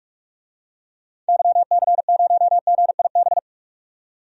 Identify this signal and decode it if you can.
Morse code, heard as a keyed tone, about 46 wpm, 700 Hz. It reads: YC0GIB
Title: DXCC-100 Call Signs - 6 Characters - 45wpm